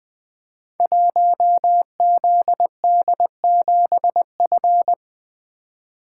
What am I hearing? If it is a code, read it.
Morse code, heard as a keyed tone, about 20 wpm, 700 Hz. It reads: 1ZD7F
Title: Code Groups - 5 Letter:Numbers - 20wpm